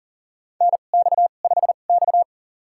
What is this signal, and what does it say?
Morse code, heard as a keyed tone, about 40 wpm, spaced as for 29 wpm, 700 Hz. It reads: NX5X
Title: USA Call Signs - 4 Characters - 2x Character Spacing - 40wpm